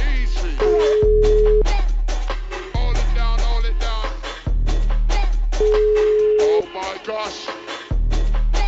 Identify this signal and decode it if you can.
Ringback tone (Europe, 425 Hz), with speech.